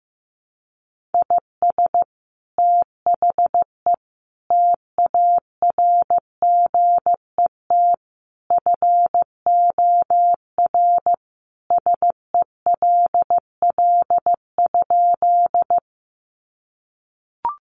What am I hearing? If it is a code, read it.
Morse code, heard as a keyed tone, about 15 wpm, 700 Hz. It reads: IS THE TARGET FOR SELL?
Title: Sentences from Top 1200 Words 15wpm